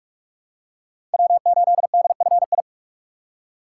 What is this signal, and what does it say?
Morse code, heard as a keyed tone, about 46 wpm, 700 Hz. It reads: W8DFI
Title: USA Call Signs - 5 Characters - 4x Character Spacing - 45wpm